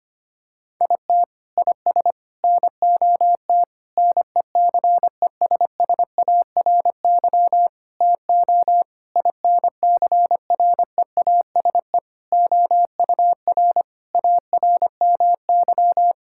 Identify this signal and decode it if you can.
Morse code, heard as a keyed tone, about 25 wpm, 700 Hz. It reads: IT IS NOT NECESSARY TO INCREASE OUR ARMY
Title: Sentences from Top 1000 Words - Review Part 6 - 25wpm